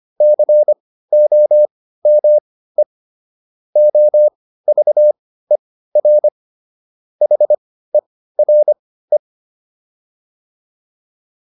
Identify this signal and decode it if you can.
Morse code, heard as a keyed tone, about 25 wpm, spaced as for 15 wpm, 600 Hz. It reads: COME OVER HERE